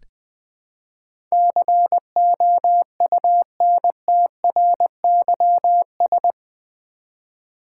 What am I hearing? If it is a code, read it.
Morse code, heard as a keyed tone, about 20 wpm, 700 Hz. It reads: COUNTRYS
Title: Build A Word - 4 Letters At A Time - 11-Letter Words - 20wpm